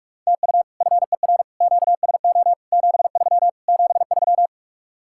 Morse code, heard as a keyed tone, about 49 wpm, 700 Hz. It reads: TU FER QSO 73 73